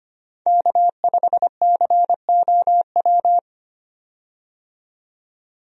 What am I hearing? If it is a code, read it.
Morse code, heard as a keyed tone, about 25 wpm, 700 Hz. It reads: K5COW
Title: USA Call Signs - 5 Characters - 25wpm